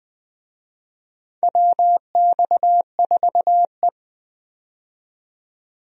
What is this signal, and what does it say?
Morse code, heard as a keyed tone, about 20 wpm, 700 Hz. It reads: WX4E
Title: USA Call Signs - 4 Characters - 20wpm